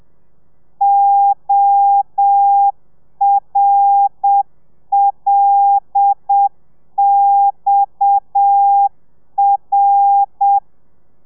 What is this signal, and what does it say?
Morse code, heard as a keyed tone, about 7 wpm, 800 Hz. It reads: ORLXR
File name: telegrafia / i2tfj / Lez7 30